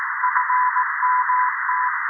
Morse code, heard as a keyed tone, about 18 wpm, 1000 Hz, with noise. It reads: FMT
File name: 0340z-fmt k5cm 80m by pa0pje